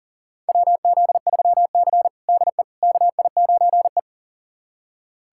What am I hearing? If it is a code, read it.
Morse code, heard as a keyed tone, about 40 wpm, 700 Hz. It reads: WZ3C DE KI9E
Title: USA Callsign DE Callsign - 4 Characters - 40wpm